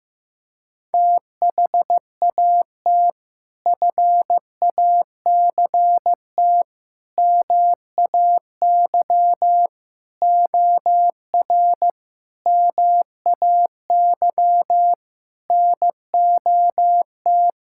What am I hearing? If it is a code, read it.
Morse code, heard as a keyed tone, about 15 wpm, 700 Hz. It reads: THAT FACT MAY OR MAY NOT
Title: Sentences from Top 1000 Words - Review Part 6 - 15wpm